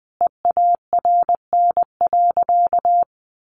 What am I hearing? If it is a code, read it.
Morse code, heard as a keyed tone, about 20 wpm, 700 Hz. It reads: EARN.
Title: Sentences from Top 1500 Words 20wpm